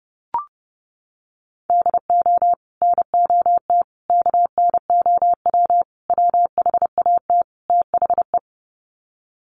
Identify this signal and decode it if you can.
Morse code, heard as a keyed tone, about 30 wpm, 700 Hz. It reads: DO NOT KNOW WHAT THE
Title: Sets of 5 Words 30wpm